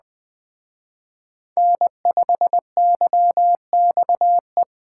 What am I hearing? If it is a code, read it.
Morse code, heard as a keyed tone, about 21 wpm, 700 Hz. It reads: N5YXE